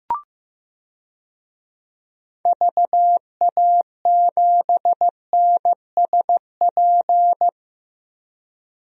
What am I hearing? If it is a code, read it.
Morse code, heard as a keyed tone, about 15 wpm, 700 Hz. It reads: VA7NSP
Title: DXCC-100 Call Signs - 6 Characters - 15wpm